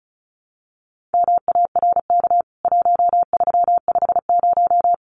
Morse code, heard as a keyed tone, about 34 wpm, 700 Hz. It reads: MARK 1350
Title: CWT Exchange 35wpm